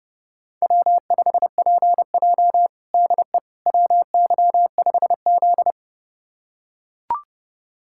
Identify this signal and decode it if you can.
Morse code, heard as a keyed tone, about 32 wpm, 700 Hz. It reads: W5PJ DE WY5Z
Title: USA Callsign DE Callsign - 4 Characters - 30wpm